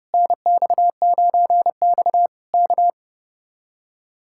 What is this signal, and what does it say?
Morse code, heard as a keyed tone, about 30 wpm, 700 Hz. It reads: NX9X K
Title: Speed-Racing Common QSO Phrases - 30wpm